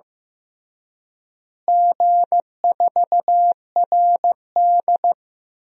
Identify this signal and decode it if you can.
Morse code, heard as a keyed tone, about 16 wpm, 700 Hz. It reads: G4RD
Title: DXCC-100 Call Signs - 4 Characters - Repeated 3x - 15wpm